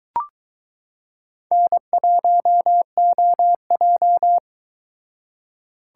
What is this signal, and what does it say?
Morse code, heard as a keyed tone, about 22 wpm, 700 Hz. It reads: N1OJ